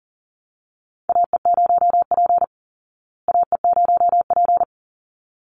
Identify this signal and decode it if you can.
Morse code, heard as a keyed tone, about 38 wpm, 700 Hz. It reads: AE0P AE0P
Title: Call Signs Repeated 3x - 40wpm